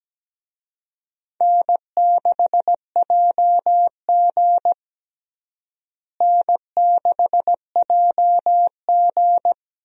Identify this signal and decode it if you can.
Morse code, heard as a keyed tone, about 17 wpm, 700 Hz. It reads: N6JG N6JG
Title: USA Call Signs - 4 Characters - Repeated 2x - 17wpm